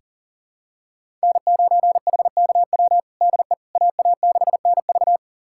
Morse code, heard as a keyed tone, about 40 wpm, 700 Hz. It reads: N9HKW DE AA6NV
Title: USA Callsign DE Callsign - 5 Characters - 40wpm